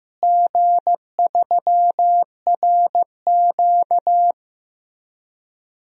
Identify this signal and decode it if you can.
Morse code, heard as a keyed tone, about 15 wpm, 700 Hz. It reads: G3RQ
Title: DXCC-100 Call Signs - 4 Characters - Repeated 3x - 15wpm